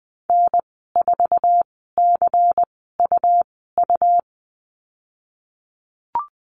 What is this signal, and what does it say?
Morse code, heard as a keyed tone, about 20 wpm, spaced as for 15 wpm, 700 Hz. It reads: N4CUU
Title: 128 Lesson 21g - Number 6 - Call Signs 20wpm